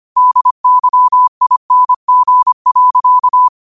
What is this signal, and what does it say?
Morse code, heard as a keyed tone, about 25 wpm, 1000 Hz. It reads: DYING.